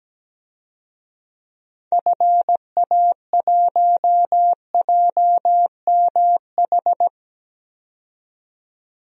Morse code, heard as a keyed tone, about 17 wpm, 700 Hz. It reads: FA1JMH
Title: DXCC-100 Call Signs - 6 Characters - 17wpm